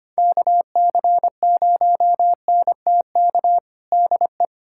Morse code, heard as a keyed tone, about 25 wpm, 700 Hz. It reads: KC0NTK DE
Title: USA Callsign DE Callsign 25wpm